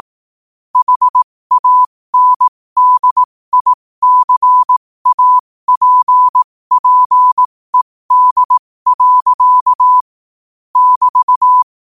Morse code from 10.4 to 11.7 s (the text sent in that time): <BT>